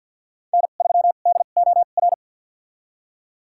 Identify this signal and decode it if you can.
Morse code, heard as a keyed tone, about 51 wpm, spaced as for 36 wpm, 700 Hz. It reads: N3DOR
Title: USA Call Signs - 5 Characters - 2x Character Spacing - 50wpm